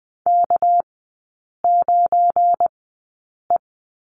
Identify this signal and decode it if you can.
Morse code, heard as a keyed tone, about 20 wpm, spaced as for 8 wpm, 700 Hz. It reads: K9E
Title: 205 Lesson 33e - Letter j - Three Characters 20wpm